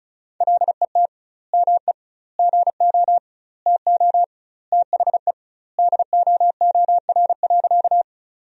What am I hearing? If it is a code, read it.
Morse code, heard as a keyed tone, about 36 wpm, 700 Hz. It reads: LET ME GO TO THE DOOR.
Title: 2x Word-Spacing - Sentences from Top 300 Words - 35wpm